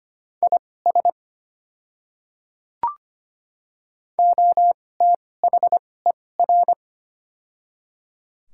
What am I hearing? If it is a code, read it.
Morse code, heard as a keyed tone, about 25 wpm, spaced as for 18 wpm, 690 Hz. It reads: IS OTHER
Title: 2x Character Spacing - Top 100 Words - 25wpm